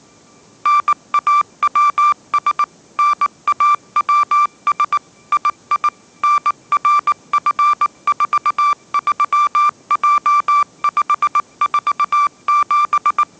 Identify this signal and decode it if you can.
Morse code, heard as a keyed tone, about 19 wpm, 1180 Hz. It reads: NAWS NAWS II NRF43J547